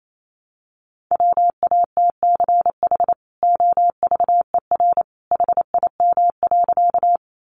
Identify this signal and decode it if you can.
Morse code, heard as a keyed tone, about 28 wpm, 700 Hz. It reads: WATCH OVER HIM.
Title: Sentences from Top 300 Words 28wpm